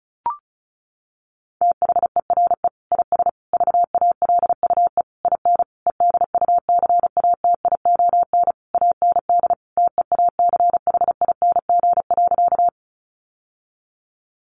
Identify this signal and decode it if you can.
Morse code, heard as a keyed tone, about 34 wpm, 700 Hz. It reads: THERE IS VALUE IN EDUCATION AND TEACHING.
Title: Sentences from Top 1100 Words 35wpm